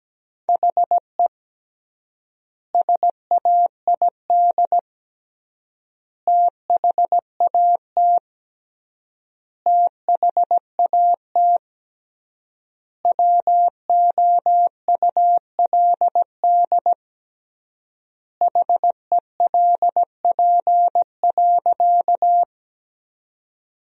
Morse code, heard as a keyed tone, about 17 wpm, 700 Hz. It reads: HE SAID THAT THAT WOULD HELP.
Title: 3x Word-Spacing - Sentences from Top 200 Words - 17wpm